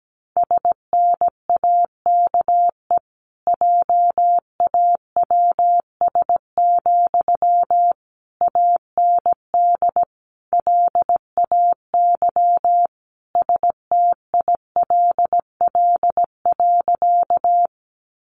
Morse code, heard as a keyed tone, about 17 wpm, 700 Hz. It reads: SNAKE JAWS, AND LAY STILL.
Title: A Sound Of Thunder 17wpm